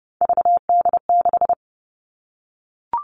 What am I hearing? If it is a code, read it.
Morse code, heard as a keyed tone, about 30 wpm, 700 Hz. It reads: VD6
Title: All Cut Numbers - 3-Digit 30wpm